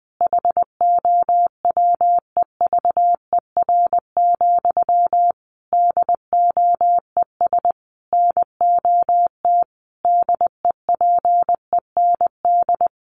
Morse code, heard as a keyed tone, about 20 wpm, 700 Hz. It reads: HOWEVER, DOES NOT DEPEND
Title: Sentences from Top 1300 Words 20wpm